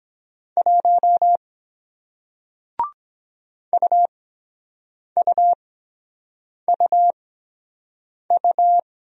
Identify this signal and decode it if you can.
Morse code, heard as a keyed tone, about 23 wpm, spaced as for 7 wpm, 700 Hz. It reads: 1 UUUU